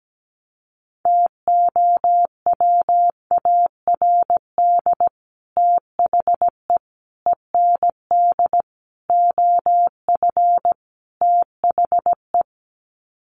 Morse code, heard as a keyed tone, about 17 wpm, 700 Hz. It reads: TOWARD THE END OF THE
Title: Sets of 5 Words 17wpm